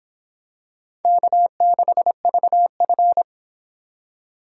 Morse code, heard as a keyed tone, about 25 wpm, 700 Hz. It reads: K6VF